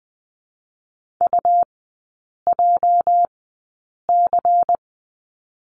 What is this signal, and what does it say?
Morse code, heard as a keyed tone, about 20 wpm, spaced as for 8 wpm, 700 Hz. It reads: UJC
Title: 205 Lesson 33e - Letter j - Three Characters 20wpm